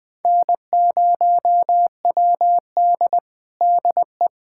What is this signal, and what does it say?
Morse code, heard as a keyed tone, about 20 wpm, 700 Hz. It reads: N0WD DE
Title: USA Callsign DE Callsign - 4 Characters - 20wpm